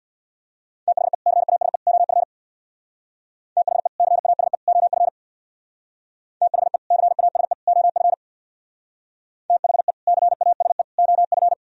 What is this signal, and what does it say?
Morse code, heard as a keyed tone, about 62 wpm, 700 Hz. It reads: THE CASE OF THE CASE OF THE CASE OF THE CASE OF
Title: Speed-Racing - Sets of 3 Words 50wpm